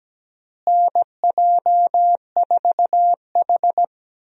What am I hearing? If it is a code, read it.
Morse code, heard as a keyed tone, about 17 wpm, 700 Hz. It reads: NJ4H
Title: USA Call Signs - 4 Characters - Repeated 2x - 17wpm